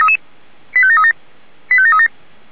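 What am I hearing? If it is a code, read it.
Five-tone selective call (ZVEI), heard as quick runs of five tones, tones of about 70 ms. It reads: 85737 85737
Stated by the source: ZVEI five-tone selective call received by SDR.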